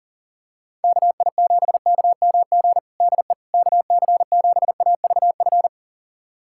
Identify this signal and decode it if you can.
Morse code, heard as a keyed tone, about 40 wpm, 700 Hz. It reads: KI7KMG DE KC7AVF
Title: USA Callsign DE Callsign - 6 Characters - 40wpm